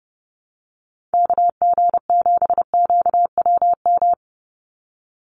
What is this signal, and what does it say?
Morse code, heard as a keyed tone, about 30 wpm, 700 Hz. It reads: KG7QWM
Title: USA Call Signs 30wpm